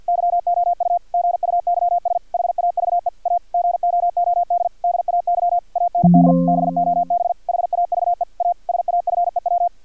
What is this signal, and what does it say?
Morse code, heard as a keyed tone, about 50 wpm, 700 Hz. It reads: YOU GUYS HAVE A GOOD DAY AND BOB HAVE A SAFEW
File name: archived / CRSnet